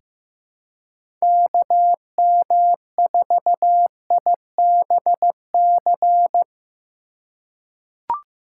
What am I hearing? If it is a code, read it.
Morse code, heard as a keyed tone, about 16 wpm, 700 Hz. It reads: KM4IBC
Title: USA Call Signs - 6 Characters - 3x Character Spacing - 15wpm